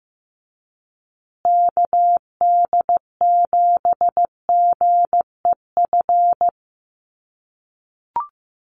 Morse code, heard as a keyed tone, about 15 wpm, 700 Hz. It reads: KD7GEF
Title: USA Call Signs Repeated 3x 15wpm